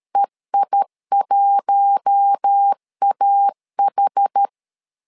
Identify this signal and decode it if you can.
Morse code, heard as a keyed tone, about 13 wpm, 800 Hz. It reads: EI1AH
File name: LEZIONI / 5 LBVC / 5 LBV 12wpm 1855